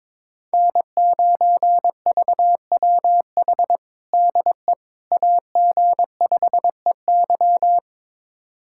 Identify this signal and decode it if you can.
Morse code, heard as a keyed tone, about 22 wpm, 700 Hz. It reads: N9VWH DE AG5EY
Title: USA Callsign DE Callsign - 5 Characters - 22wpm